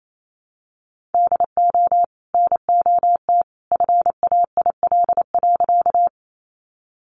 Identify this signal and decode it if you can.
Morse code, heard as a keyed tone, about 28 wpm, 700 Hz. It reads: DO NOT FAIL.
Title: Sentences from Top 1100 Words 28wpm